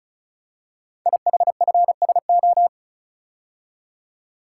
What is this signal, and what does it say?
Morse code, heard as a keyed tone, about 36 wpm, 700 Hz. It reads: IHFSO
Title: Code Groups - 5 Letters - 35wpm